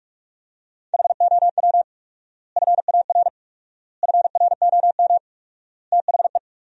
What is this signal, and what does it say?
Morse code, heard as a keyed tone, about 46 wpm, 700 Hz. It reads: HOW FAR FROM THE
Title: 4x Word-Spacing - Sentences from Top 400 Words - 45wpm